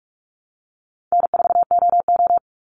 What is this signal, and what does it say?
Morse code, heard as a keyed tone, about 44 wpm, 700 Hz. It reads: N4OO
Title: Call Signs Repeated 3x - 45wpm